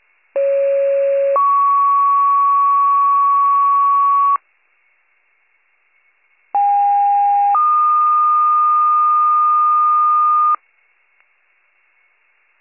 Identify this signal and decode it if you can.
Two-tone page, heard as a one-second tone, then a three-second tone, about 550 then 1120 Hz, about 810 then 1220 Hz. It is heard from 0.3 to 10.6 s.